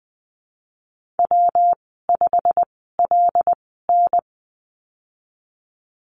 Morse code, heard as a keyed tone, about 20 wpm, spaced as for 14 wpm, 700 Hz. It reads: W5LN
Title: 207 Lesson 33g - Letter j - Call Signs 20wpm